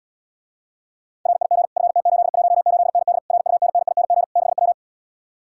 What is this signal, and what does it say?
Morse code, heard as a keyed tone, about 80 wpm, 700 Hz. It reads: RIG FT818ND RUNNING 6W